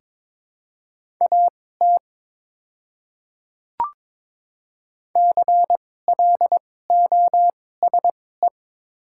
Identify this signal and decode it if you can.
Morse code, heard as a keyed tone, about 22 wpm, spaced as for 16 wpm, 700 Hz. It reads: AT CLOSE
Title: 2x Character Spacing - Top 300 Words - 22wpm